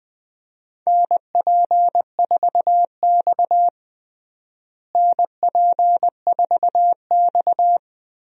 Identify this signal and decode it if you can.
Morse code, heard as a keyed tone, about 20 wpm, 700 Hz. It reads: NP4X NP4X
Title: USA Call Signs - 4 Characters - Repeated 3x - 20wpm